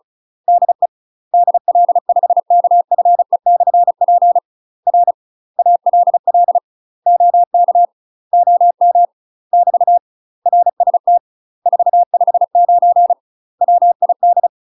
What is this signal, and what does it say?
Morse code, heard as a keyed tone, about 36 wpm, 700 Hz. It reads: DE DL5KFE/P R ALL OK OM <BT> RST 459 WID
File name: UK MASTER QSOs TRAINING / 35WPM / UK MASTER QSOs 0009